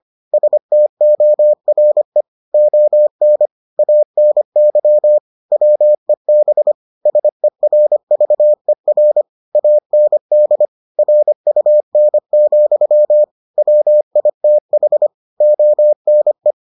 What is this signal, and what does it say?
Morse code, heard as a keyed tone, about 25 wpm, 600 Hz. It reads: STORE ON ANY WEB SERVER AND RUN, WITH ONE